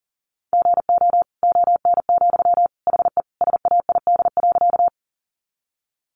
Morse code, heard as a keyed tone, about 40 wpm, 700 Hz. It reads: GO ON, HE SAID.
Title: Sentences from Top 100 Words 40wpm